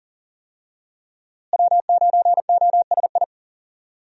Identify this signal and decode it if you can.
Morse code, heard as a keyed tone, about 40 wpm, 700 Hz. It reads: W9OSI
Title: USA Call Signs - 5 Characters - Repeated 3x - 40wpm